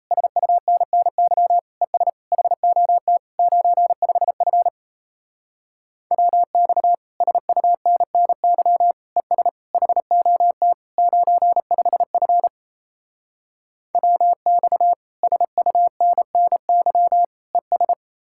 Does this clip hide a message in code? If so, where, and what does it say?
Yes: Morse code from 0.1 to 17.9 s, heard as a keyed tone, about 37 wpm, 700 Hz. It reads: SUNNY ES HOT 95F WX SUNNY ES HOT 95F WX SUNNY ES